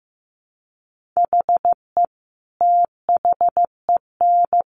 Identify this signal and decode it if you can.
Morse code, heard as a keyed tone, about 15 wpm, 700 Hz. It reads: HE THEN
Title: Sentences from Top 1400 Words 15wpm